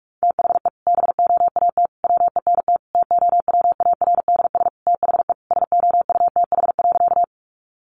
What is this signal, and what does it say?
Morse code, heard as a keyed tone, about 44 wpm, 700 Hz. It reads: THE BOAT WENT TOWARDS THE SOUTH.
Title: Sentences from Top 1100 Words 45wpm